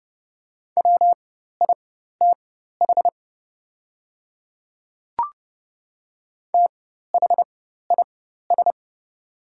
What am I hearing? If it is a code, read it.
Morse code, heard as a keyed tone, about 30 wpm, spaced as for 14 wpm, 700 Hz. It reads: WITH THIS
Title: 4x Character Spacing - Top 100 Words - 30wpm